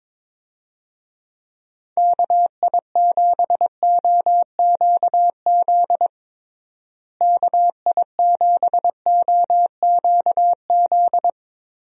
Morse code, heard as a keyed tone, about 22 wpm, 700 Hz. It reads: KI7OQZ KI7OQZ
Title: USA Call Signs - 6 Characters - Repeated 3x - 22wpm